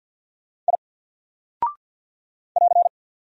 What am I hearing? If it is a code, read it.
Morse code, heard as a keyed tone, about 49 wpm, 700 Hz. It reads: I <AR>
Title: Single Character (Letter:Number:Punctuation:Prosign) 50wpm